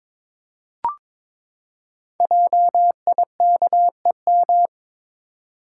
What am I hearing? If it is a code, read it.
Morse code, heard as a keyed tone, about 22 wpm, 700 Hz. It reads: JIKEM
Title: Code Groups - 5 Letters - 22wpm